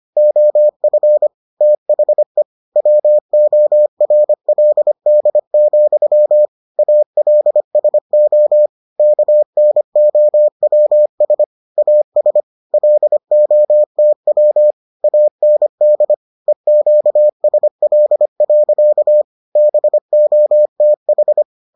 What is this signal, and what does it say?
Morse code, heard as a keyed tone, about 25 wpm, 600 Hz. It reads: OF THE WORLD, ALSO KNOWS AS LOTW AND EQSL. BOTH